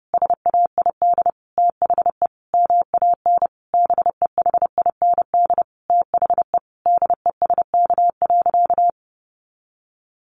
Morse code, heard as a keyed tone, about 30 wpm, 700 Hz. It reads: SAID THE MAN BEHIND THE DESK.
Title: A Sound Of Thunder 30wpm